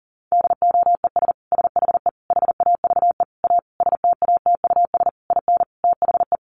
Morse code, heard as a keyed tone, about 40 wpm, 700 Hz. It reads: DOES SHE HAVE A STATUS IN THE
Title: Sentences from Top 1400 Words 40wpm